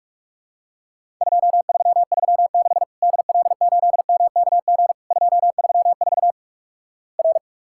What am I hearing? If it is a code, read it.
Morse code, heard as a keyed tone, about 46 wpm, 700 Hz. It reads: 2336 DL8MKG 234 R
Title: Worked All Europe DX Contest - QTC Exchange - 45wpm